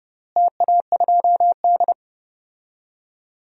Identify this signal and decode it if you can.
Morse code, heard as a keyed tone, about 32 wpm, 700 Hz. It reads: TA2D
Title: DXCC-100 Call Signs - 4 Characters - Repeated 2x - 30wpm